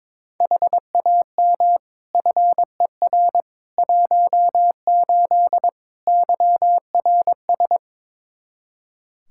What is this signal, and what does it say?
Morse code, heard as a keyed tone, about 22 wpm, 700 Hz. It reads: HAM FER 18 YRS
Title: Practice QSO - Exchange 3 - More Info - 22wpm